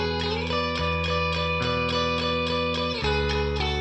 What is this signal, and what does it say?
There is no code.